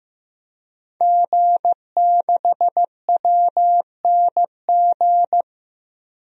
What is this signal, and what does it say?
Morse code, heard as a keyed tone, about 16 wpm, 700 Hz. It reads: G6WNG
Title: DXCC-100 Call Signs - 5 Characters - Repeated 3x - 15wpm